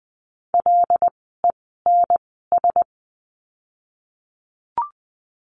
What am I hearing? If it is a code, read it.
Morse code, heard as a keyed tone, about 20 wpm, 700 Hz. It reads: L E N S
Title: 075 Lesson 13f - Letter L - Words 20wpm